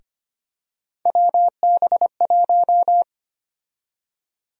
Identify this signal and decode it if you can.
Morse code, heard as a keyed tone, about 25 wpm, 700 Hz. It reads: WB1
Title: USA Call Sign Prefixes - Repeated 2x - 25wpm